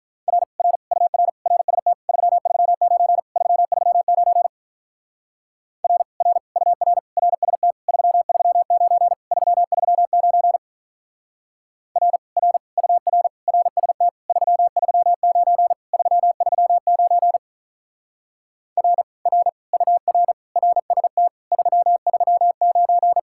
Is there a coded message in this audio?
Yes: Morse code from 0.3 to 23.2 s, heard as a keyed tone, about 47 wpm, 700 Hz. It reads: R R UR RST 339 339 R R UR RST 339 339 R R UR RST 339 339 R R UR RST 339